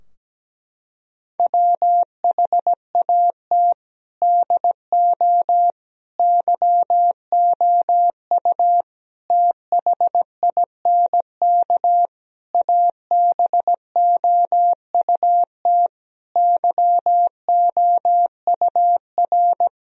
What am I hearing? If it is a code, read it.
Morse code, heard as a keyed tone, about 17 wpm, 700 Hz. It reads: WHAT DO YOU THINK ABOUT YOUR
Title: Sentences from Top 1000 Words - Review Part 4 - 17wpm